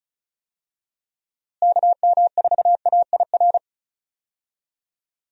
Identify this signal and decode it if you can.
Morse code, heard as a keyed tone, about 36 wpm, 700 Hz. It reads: KM4AIR
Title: USA Call Signs - 6 Characters - Repeated 2x - 35wpm